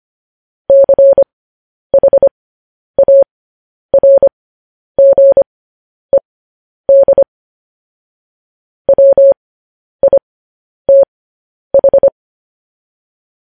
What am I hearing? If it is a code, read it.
Morse code, heard as a keyed tone, about 25 wpm, spaced as for 10 wpm, 550 Hz. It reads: CHARGED WITH
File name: podcast / News headlines-1x-25x10